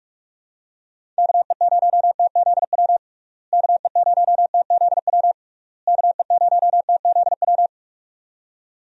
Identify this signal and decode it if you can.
Morse code, heard as a keyed tone, about 42 wpm, 700 Hz. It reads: KE0TZW KE0TZW KE0TZW
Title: USA Call Signs - 6 Characters - Repeated 3x - 45wpm